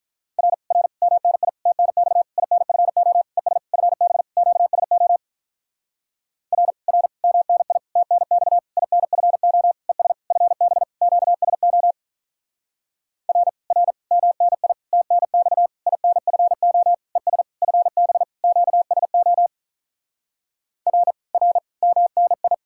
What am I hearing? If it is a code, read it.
Morse code, heard as a keyed tone, about 48 wpm, 700 Hz. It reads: R R MNI TNX INFO ES FB QSO R R MNI TNX INFO ES FB QSO R R MNI TNX INFO ES FB QSO R R MNI